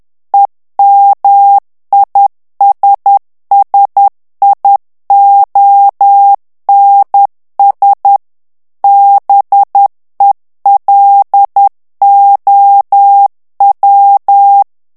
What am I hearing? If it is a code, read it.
Morse code, heard as a keyed tone, about 10 wpm, 800 Hz. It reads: EMISSIONS BELOW
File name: ANGLAIS texte clair 3 vitesse 10 mots minute